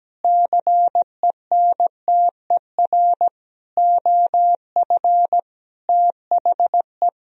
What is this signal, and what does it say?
Morse code, heard as a keyed tone, about 17 wpm, 700 Hz. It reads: CENTER OF THE
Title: Sentences from Top 1000 Words - Review Part 2 - 17wpm